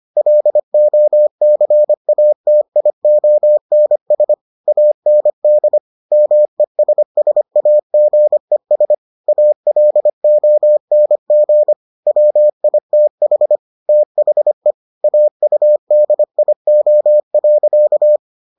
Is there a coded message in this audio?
Yes: Morse code from 0.2 to 18.2 s, heard as a keyed tone, about 25 wpm, 600 Hz. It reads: LOCATIONS AND MESSAGES ALONG WITH THE AUDIO.